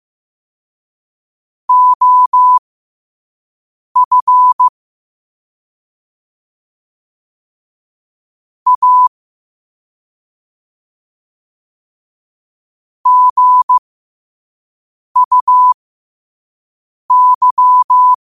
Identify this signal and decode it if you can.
Morse code, heard as a keyed tone, about 15 wpm, spaced as for 5 wpm, 1000 Hz. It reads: OF A GUY
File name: qotdfiles / 5wpm